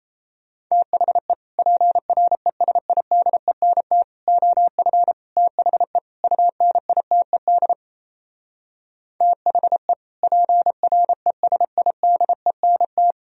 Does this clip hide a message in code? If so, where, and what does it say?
Yes: Morse code from 0.7 to 13.1 s, heard as a keyed tone, about 37 wpm, 700 Hz. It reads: THE PRESIDENT OF THE UNITED THE PRESIDENT